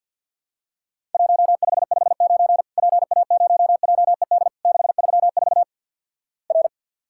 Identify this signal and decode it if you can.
Morse code, heard as a keyed tone, about 50 wpm, 700 Hz. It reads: 1559 PA0JED 634 R
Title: Worked All Europe DX Contest - QTC Exchange with Repeats - 50wpm